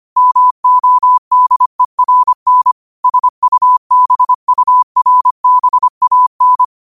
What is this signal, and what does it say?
Morse code, heard as a keyed tone, about 25 wpm, 1000 Hz. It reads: MODERN SUBURBAN